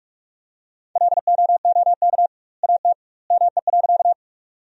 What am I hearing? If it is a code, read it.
Morse code, heard as a keyed tone, about 46 wpm, 700 Hz. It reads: LOOK AT ME.
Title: 2x Word-Spacing - Sentences from Top 200 Words - 45wpm